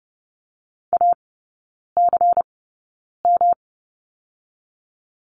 Morse code, heard as a keyed tone, about 30 wpm, 700 Hz. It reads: A C M
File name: Lesson 19e - Letter w - Three Characters 30wpm